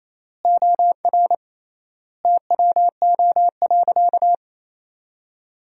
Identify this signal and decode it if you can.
Morse code, heard as a keyed tone, about 28 wpm, 700 Hz. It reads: OR TWO.
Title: 3x Word-Spacing - Sentences from Top 300 Words - 28wpm